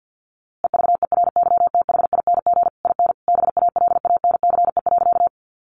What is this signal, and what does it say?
Morse code, heard as a keyed tone, about 50 wpm, 700 Hz. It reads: EVERYTHING IN BALANCE.
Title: A Sound Of Thunder 50wpm